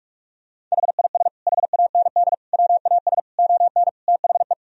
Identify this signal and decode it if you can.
Morse code, heard as a keyed tone, about 46 wpm, 700 Hz. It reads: HIS HAND WAS ON THE
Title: Sentences from Top 1000 Words - Review Part 1 - 45wpm